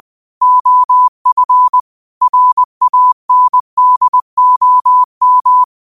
Morse code, heard as a keyed tone, about 20 wpm, 1000 Hz. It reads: OF RANDOM